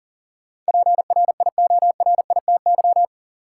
Morse code, heard as a keyed tone, about 40 wpm, 700 Hz. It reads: PRIORITY
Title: Sentences from Top 1600 Words - New Word Drill Alt - 40wpm